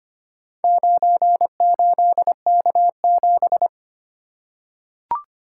Morse code, heard as a keyed tone, about 25 wpm, 700 Hz. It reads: 98K7
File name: practice / 25wpm / Four Letter-Number Combinations - 25wpm